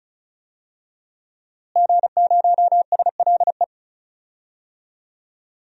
Morse code, heard as a keyed tone, about 34 wpm, 700 Hz. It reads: G0SLE
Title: DXCC-100 Call Signs - 5 Characters - Repeated 3x - 35wpm